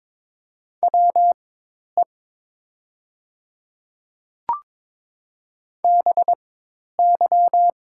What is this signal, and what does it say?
Morse code, heard as a keyed tone, about 22 wpm, spaced as for 10 wpm, 700 Hz. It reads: WE BY
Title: 4x Character Spacing - Top 100 Words - 22wpm